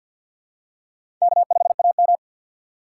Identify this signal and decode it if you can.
Morse code, heard as a keyed tone, about 53 wpm, 700 Hz. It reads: K5AM
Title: K1USN SST QSO Practice - 50wpm